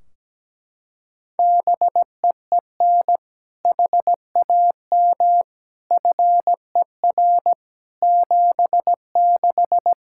Morse code, heard as a keyed tone, about 17 wpm, 700 Hz. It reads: BEEN HAM FER 76
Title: Practice QSO - Exchange 3 - More Info - 17wpm